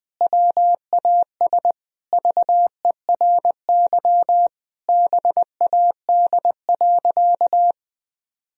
Morse code, heard as a keyed tone, about 20 wpm, 700 Hz. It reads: WAS VERY BAD.
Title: Sentences from Top 1000 Words - Review Part 3 - 20wpm